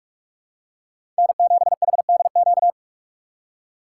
Morse code, heard as a keyed tone, about 41 wpm, 700 Hz. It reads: N7HDQ
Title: USA Call Signs - 5 Characters - 4x Character Spacing - 45wpm